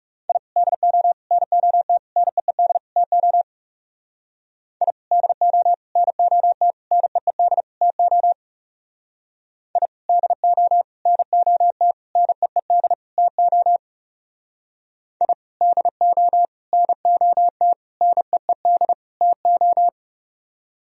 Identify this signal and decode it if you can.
Morse code, heard as a keyed tone, about 34 wpm, 700 Hz. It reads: I DO NOT NEED TO I DO NOT NEED TO I DO NOT NEED TO I DO NOT NEED TO
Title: Speed-Racing - Sets of 5 Words 30wpm